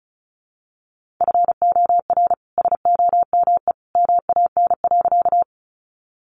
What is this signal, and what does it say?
Morse code, heard as a keyed tone, about 34 wpm, 700 Hz. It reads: FOR SOME MAN.
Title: Sentences from Top 200 Words 35wpm